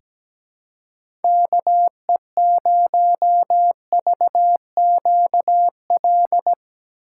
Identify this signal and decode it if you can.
Morse code, heard as a keyed tone, about 17 wpm, 700 Hz. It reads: KE0VQL
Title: USA Call Signs - 6 Characters - 4x Character Spacing - 17wpm